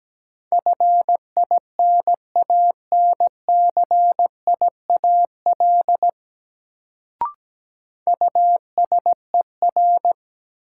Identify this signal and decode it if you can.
Morse code, heard as a keyed tone, about 17 wpm, 700 Hz. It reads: FINANCIAL USER
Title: Sentences from Top 1000 Words - Review Part 5 - New Word Drill - 17wpm